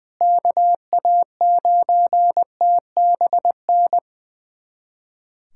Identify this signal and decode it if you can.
Morse code, heard as a keyed tone, about 20 wpm, 700 Hz. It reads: KA9TBN
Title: USA Callsign DE Callsign 20wpm